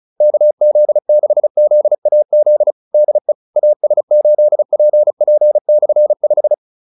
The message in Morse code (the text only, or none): KZ6ZAZ DE AS8PP/5